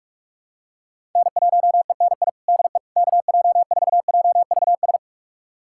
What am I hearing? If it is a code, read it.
Morse code, heard as a keyed tone, about 46 wpm, 700 Hz. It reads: N1ENI DE KJ4JVS
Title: USA Callsign DE Callsign 45wpm